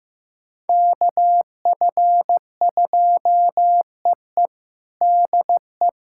Morse code, heard as a keyed tone, about 16 wpm, 700 Hz. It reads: KF2EE DE
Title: USA Callsign DE Callsign - 5 Characters - 15wpm